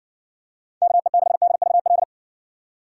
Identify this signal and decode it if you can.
Morse code, heard as a keyed tone, about 60 wpm, 700 Hz. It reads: KE6DVL